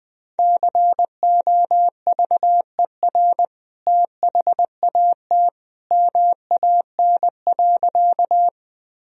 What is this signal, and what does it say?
Morse code, heard as a keyed tone, about 21 wpm, 700 Hz. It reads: COVER THAT MAN.
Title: Sentences from Top 1000 Words - Review Part 1 - 20wpm